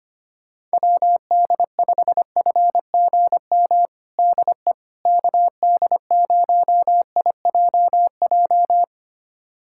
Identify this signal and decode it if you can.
Morse code, heard as a keyed tone, about 25 wpm, 700 Hz. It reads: WD5FGM DE KD0IJJ
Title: USA Callsign DE Callsign - 6 Characters - 25wpm